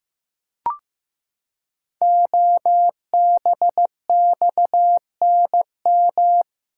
Morse code, heard as a keyed tone, about 16 wpm, 700 Hz. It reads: OBXNM
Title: Code Groups - 5 Letters - 15wpm